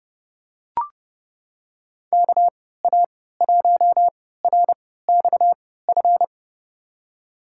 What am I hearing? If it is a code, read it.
Morse code, heard as a keyed tone, about 30 wpm, spaced as for 17 wpm, 700 Hz. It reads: KA1RXF
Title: USA Call Signs - 6 Characters - 3x Character Spacing - 30wpm